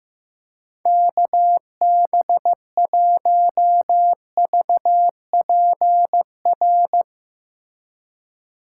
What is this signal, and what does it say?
Morse code, heard as a keyed tone, about 15 wpm, 700 Hz. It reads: KB1VPR
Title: Speed Racing - USA 2x3 Call Signs 15wpm